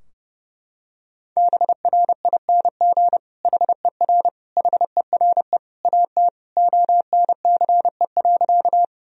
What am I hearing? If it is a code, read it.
Morse code, heard as a keyed tone, about 30 wpm, 700 Hz. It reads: BRING HER HERE AT ONCE.
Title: Sentences from Top 1000 Words - Review Part 3 - 30wpm